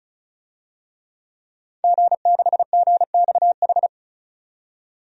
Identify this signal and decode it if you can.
Morse code, heard as a keyed tone, about 35 wpm, 700 Hz. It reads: G6GXH
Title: DXCC-100 Call Signs - 5 Characters - 35wpm